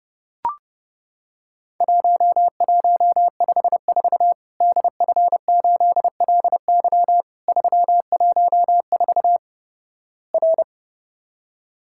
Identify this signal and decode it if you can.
Morse code, heard as a keyed tone, about 31 wpm, 700 Hz. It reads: 1154 DF8LY 314 R
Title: Worked All Europe DX Contest - QTC Exchange - 30wpm